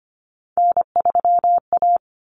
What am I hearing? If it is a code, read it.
Morse code, heard as a keyed tone, about 25 wpm, 700 Hz. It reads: N3A
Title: Common Cut Numbers - 3-Digit 25wpm